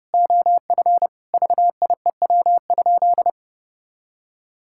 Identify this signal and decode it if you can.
Morse code, heard as a keyed tone, about 32 wpm, 700 Hz. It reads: OF VIEW?
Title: Sentences from Top 1000 Words - Review Part 6 - 30wpm